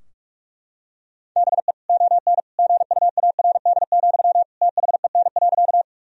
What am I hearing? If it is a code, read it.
Morse code, heard as a keyed tone, about 41 wpm, 700 Hz. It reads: BE ON GUARD, THEN.
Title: Sentences from Top 1000 Words - Review Part 4 - 45wpm